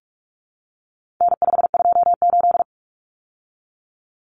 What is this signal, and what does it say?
Morse code, heard as a keyed tone, about 44 wpm, 700 Hz. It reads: N528
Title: Common Cut Numbers - 4-Digit 45wpm